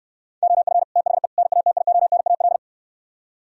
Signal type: Morse code

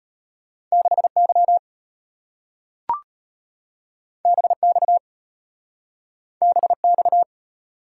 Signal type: Morse code

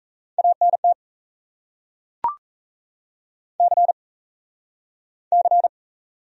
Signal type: Morse code

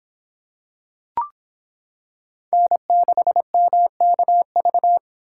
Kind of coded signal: Morse code

